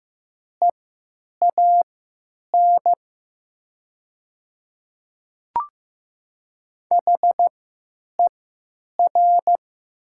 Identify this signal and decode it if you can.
Morse code, heard as a keyed tone, about 15 wpm, spaced as for 9 wpm, 700 Hz. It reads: EAN HER
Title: 3x Character Spacing - Top 200 Words - 15wpm